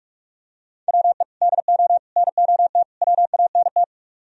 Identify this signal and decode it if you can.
Morse code, heard as a keyed tone, about 45 wpm, 700 Hz. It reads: WE DO NOT WANT